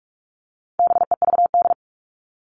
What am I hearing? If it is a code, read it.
Morse code, heard as a keyed tone, about 44 wpm, 700 Hz. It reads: BEVD
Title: All Cut Numbers - 4-Digit 45wpm